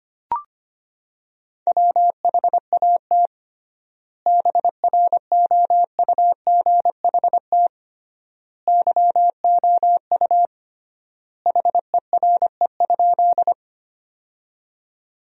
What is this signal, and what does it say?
Morse code, heard as a keyed tone, about 25 wpm, 700 Hz. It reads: WHAT BROUGHT YOU HERE?
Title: 3x Word-Spacing - Sentences from Top 500 Words - 25wpm